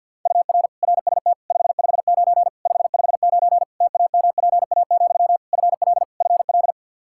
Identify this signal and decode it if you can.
Morse code, heard as a keyed tone, about 51 wpm, 700 Hz. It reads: UR RST 559 559 TAMPA, FL FL
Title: Practice QSO - Exchange 1 - Beginning - 50wpm